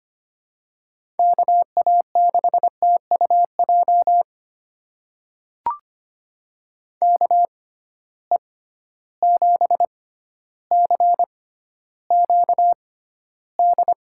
Morse code, heard as a keyed tone, about 25 wpm, spaced as for 13 wpm, 700 Hz. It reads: KA6TUJ KE7CQD
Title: USA Call Signs - 6x Character Spacing - 25wpm